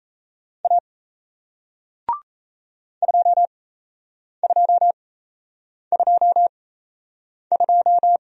Morse code, heard as a keyed tone, about 38 wpm, spaced as for 8 wpm, 700 Hz. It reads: A 2222